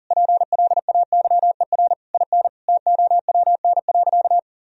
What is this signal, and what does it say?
Morse code, heard as a keyed tone, about 40 wpm, 700 Hz. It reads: PLAYER IN TOWN.
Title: Sentences from Top 1000 Words - Review Part 4 - 40wpm